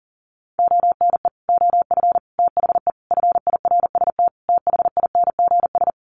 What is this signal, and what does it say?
Morse code, heard as a keyed tone, about 40 wpm, 700 Hz. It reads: ONE OF THE FIRST THINGS
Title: Sets of 5 Words Encore 40wpm